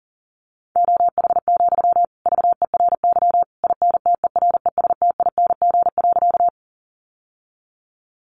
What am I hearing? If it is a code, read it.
Morse code, heard as a keyed tone, about 40 wpm, 700 Hz. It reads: OH, VERY INTERESTING.
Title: Sentences from Top 1000 Words 40wpm